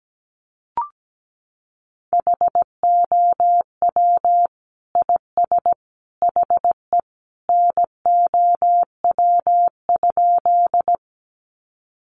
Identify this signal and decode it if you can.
Morse code, heard as a keyed tone, about 17 wpm, 700 Hz. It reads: HOW IS HE NOW?
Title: Sentences from Top 100 Words 17wpm